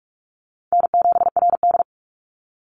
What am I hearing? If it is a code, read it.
Morse code, heard as a keyed tone, about 44 wpm, 700 Hz. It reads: N7RD
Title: Call Signs - 45wpm